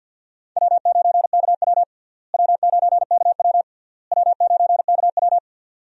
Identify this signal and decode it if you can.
Morse code, heard as a keyed tone, about 50 wpm, 700 Hz. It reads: W9KW W9KW W9KW
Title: USA Call Signs - 4 Characters - Repeated 3x - 50wpm